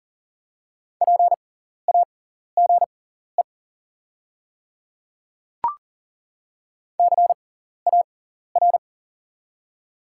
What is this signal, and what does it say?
Morse code, heard as a keyed tone, about 40 wpm, spaced as for 14 wpm, 700 Hz. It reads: PAGE CAR
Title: 6x Character Spacing - Top 300 Words - 40wpm